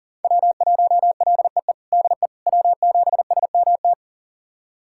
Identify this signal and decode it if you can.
Morse code, heard as a keyed tone, about 37 wpm, 700 Hz. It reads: W1LEE DE W7SMT
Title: USA Callsign DE Callsign - 5 Characters - 40wpm